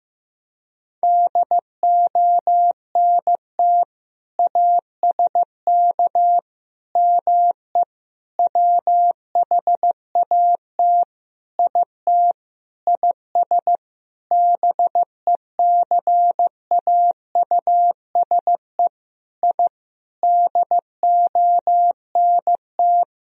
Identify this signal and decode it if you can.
Morse code, heard as a keyed tone, about 16 wpm, 700 Hz. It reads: DONT ASK ME WHAT IT IS BECAUSE I DONT
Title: Sentences from Top 1000 Words - Review Part 3 - 15wpm